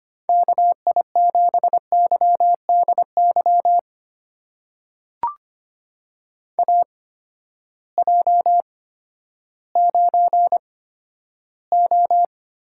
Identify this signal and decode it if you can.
Morse code, heard as a keyed tone, about 25 wpm, spaced as for 12 wpm, 700 Hz. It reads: KI7YDY AJ9O
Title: USA Call Signs - 8x Character Spacing - 25wpm